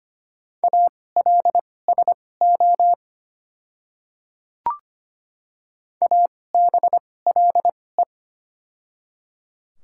Morse code, heard as a keyed tone, about 25 wpm, spaced as for 18 wpm, 700 Hz. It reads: ALSO ABLE
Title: 2x Character Spacing - Top 500 Words - 25wpm